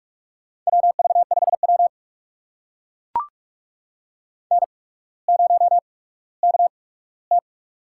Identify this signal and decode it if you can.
Morse code, heard as a keyed tone, about 44 wpm, spaced as for 19 wpm, 690 Hz. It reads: WV5W N0KT
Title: USA Call Signs - 4 Characters - 8x Character Spacing - 45wpm